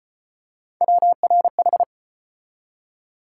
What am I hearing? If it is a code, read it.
Morse code, heard as a keyed tone, about 33 wpm, 700 Hz. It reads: WRH